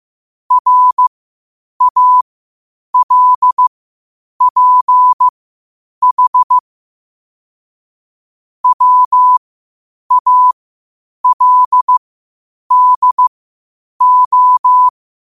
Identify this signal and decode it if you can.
Morse code, heard as a keyed tone, about 15 wpm, spaced as for 8 wpm, 1000 Hz. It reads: RALPH WALDO
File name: qotdfiles / 8wpm